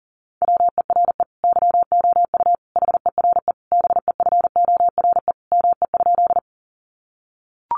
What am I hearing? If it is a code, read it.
Morse code, heard as a keyed tone, about 40 wpm, 700 Hz. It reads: WERE YOU HERE BEFORE ME?
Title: Sentences from Top 200 Words 40wpm